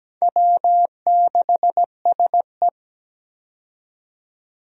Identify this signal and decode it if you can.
Morse code, heard as a keyed tone, about 17 wpm, 700 Hz. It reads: W6SE
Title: USA Call Signs - 4 Characters - 17wpm